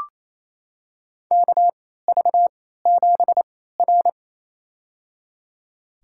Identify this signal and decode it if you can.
Morse code, heard as a keyed tone, about 28 wpm, spaced as for 16 wpm, 700 Hz. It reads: KV7R
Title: USA Call Signs - 4 Characters - 3x Character Spacing - 28wpm